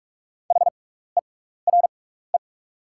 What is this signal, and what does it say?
Morse code, heard as a keyed tone, about 45 wpm, spaced as for 16 wpm, 690 Hz. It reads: HERE
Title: 6x Character Spacing - Top 200 Words - 45wpm